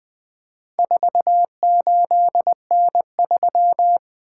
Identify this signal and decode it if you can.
Morse code, heard as a keyed tone, about 20 wpm, 700 Hz. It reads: 48N3
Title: Four Letter-Number Combinations - 20wpm